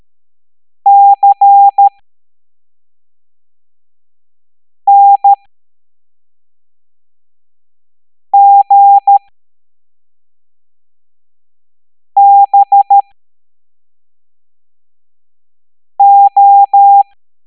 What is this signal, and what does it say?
Morse code, heard as a keyed tone, about 13 wpm, spaced as for 3 wpm, 800 Hz. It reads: CNGBO
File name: hamdata / L07-EX03-HQP